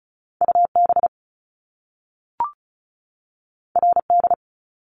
Morse code, heard as a keyed tone, about 36 wpm, 700 Hz. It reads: U B R D
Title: Top Two Letter Combinations in Words 35wpm